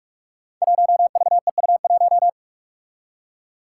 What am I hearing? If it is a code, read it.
Morse code, heard as a keyed tone, about 44 wpm, 700 Hz. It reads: 1VEU1
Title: Five Challenging Letter-Number Combinations 45wpm